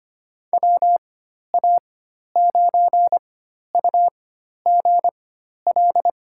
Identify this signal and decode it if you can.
Morse code, heard as a keyed tone, about 25 wpm, spaced as for 12 wpm, 700 Hz. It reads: WA9UGL
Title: USA Call Signs - 4x Character Spacing - 25wpm